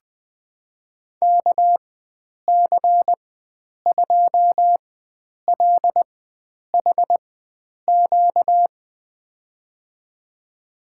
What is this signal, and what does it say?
Morse code, heard as a keyed tone, about 20 wpm, spaced as for 9 wpm, 700 Hz. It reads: KC2LHQ
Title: USA Call Signs - 6 Characters - 4x Character Spacing - 20wpm